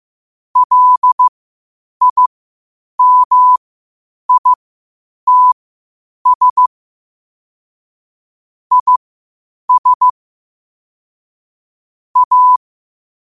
Morse code, heard as a keyed tone, about 15 wpm, spaced as for 8 wpm, 1000 Hz. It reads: LIMITS IS A
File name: qotdfiles / 8wpm